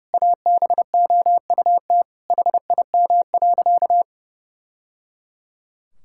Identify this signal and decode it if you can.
Morse code, heard as a keyed tone, about 30 wpm, 700 Hz. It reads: ABOUT HIM.
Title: Sentences from Top 1000 Words - Review Part 5 - 30wpm